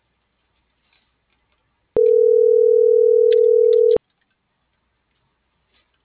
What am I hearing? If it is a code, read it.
Ringback tone (North America), 440 + 480 Hz.